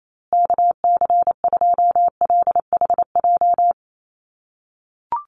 Morse code, heard as a keyed tone, about 28 wpm, 700 Hz. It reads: KC2LHJ
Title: USA Call Signs Repeated 2x 28wpm